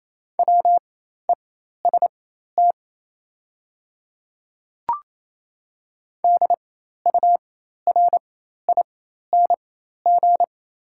Morse code, heard as a keyed tone, about 27 wpm, spaced as for 13 wpm, 690 Hz. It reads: WEST DURING
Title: 4x Character Spacing - Top 500 Words - 28wpm